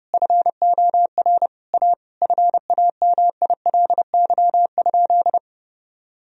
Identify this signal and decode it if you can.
Morse code, heard as a keyed tone, about 30 wpm, 700 Hz. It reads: FOR A FAMILY?
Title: Sentences from Top 1000 Words - Review Part 4 - 30wpm